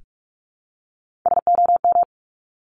Morse code, heard as a keyed tone, about 46 wpm, 700 Hz. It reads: SOM
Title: Top Three Letter Combinations in Words 45wpm